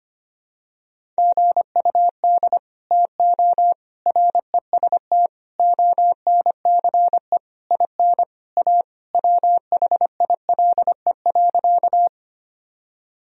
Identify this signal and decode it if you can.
Morse code, heard as a keyed tone, about 25 wpm, 700 Hz. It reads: GUD TO REST ONCE IN A WHILE.
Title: Sentences from Top 500 Words and Abbreviations 25wpm